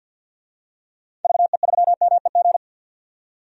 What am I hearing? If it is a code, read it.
Morse code, heard as a keyed tone, about 51 wpm, 700 Hz. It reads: VE3MEG
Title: DXCC-100 Call Signs - 6 Characters - 50wpm